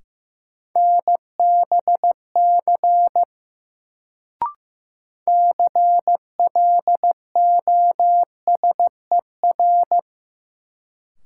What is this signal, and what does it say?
Morse code, heard as a keyed tone, about 15 wpm, 700 Hz. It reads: NBC CLOSER
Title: Sets of 5 Words Encore - New Word Drill - 15wpm